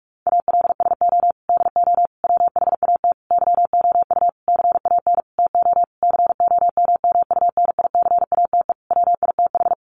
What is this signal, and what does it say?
Morse code, heard as a keyed tone, about 44 wpm, 700 Hz. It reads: ALSO DO WHAT YOU CAN TO COMMUNICATE WITH
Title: Sentences from Top 1300 Words 45wpm